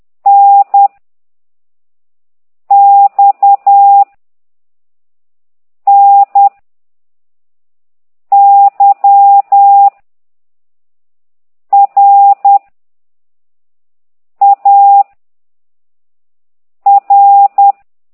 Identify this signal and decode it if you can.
Morse code, heard as a keyed tone, about 10 wpm, 800 Hz. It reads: N X N Y R A R
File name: ct2idw / exerc9